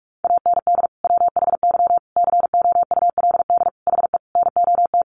Morse code, heard as a keyed tone, about 44 wpm, 700 Hz. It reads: AND WHY COULD HE NOT
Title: Sentences from Top 600 Words 45wpm